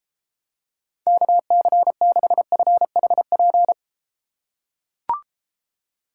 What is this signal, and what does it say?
Morse code, heard as a keyed tone, about 34 wpm, 700 Hz. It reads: KC6FHP